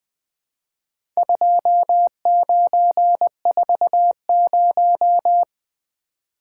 Morse code, heard as a keyed tone, about 20 wpm, 700 Hz. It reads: 2940
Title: Four Letter-Number Combinations - 20wpm